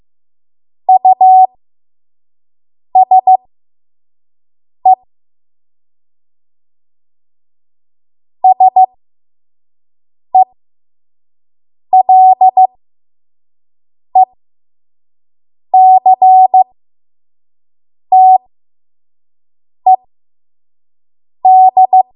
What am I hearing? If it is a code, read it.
Morse code, heard as a keyed tone, about 15 wpm, spaced as for 5 wpm, 750 Hz. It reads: USE SELECTED
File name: Morse / 05WPM